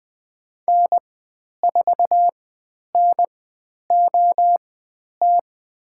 Morse code, heard as a keyed tone, about 20 wpm, spaced as for 10 wpm, 700 Hz. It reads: N4NOT
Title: 109 Lesson 18g - Letter m - Call Signs 20wpm